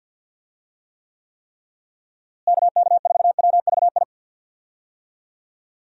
Morse code, heard as a keyed tone, about 46 wpm, 700 Hz. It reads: KK4WVI
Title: USA Call Signs - 6 Characters - Repeated 3x - 50wpm